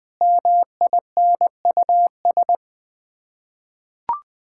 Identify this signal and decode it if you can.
Morse code, heard as a keyed tone, about 19 wpm, 700 Hz. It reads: MINUS
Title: Sets of 8 Words - 20wpm